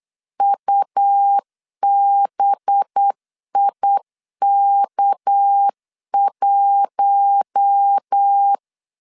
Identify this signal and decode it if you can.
Morse code, heard as a keyed tone, about 8 wpm, 800 Hz. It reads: UBIK1
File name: LEZIONI / 6 PKX0 / 6 PK 8wpm 3003